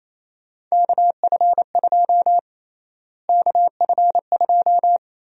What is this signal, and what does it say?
Morse code, heard as a keyed tone, about 27 wpm, 700 Hz. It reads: KF2 KF2
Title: USA Call Sign Prefixes - Repeated 2x - 28wpm